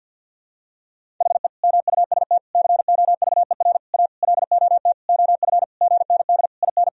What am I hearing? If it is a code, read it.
Morse code, heard as a keyed tone, about 47 wpm, 700 Hz. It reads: HE MUST COVER A LOT OF GND IN
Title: Sentences from Top 500 Words and Abbreviations 50wpm